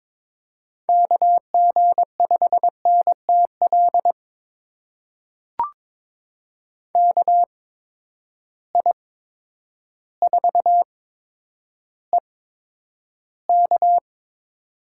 Morse code, heard as a keyed tone, about 22 wpm, spaced as for 10 wpm, 700 Hz. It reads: KG5NTL KI4EK
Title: USA Call Signs - 6 Characters - 8x Character Spacing - 22wpm